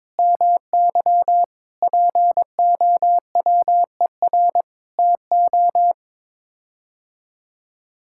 Morse code, heard as a keyed tone, about 22 wpm, 700 Hz. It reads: MY POWER TO
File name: practice / 22wpm / Sets of 8 Words - 22wpm